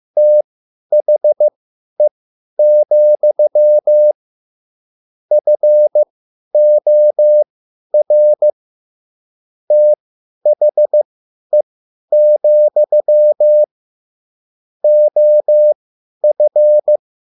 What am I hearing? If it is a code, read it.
Morse code, heard as a keyed tone, about 15 wpm, spaced as for 10 wpm, 600 Hz. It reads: THE, FOR THE, OF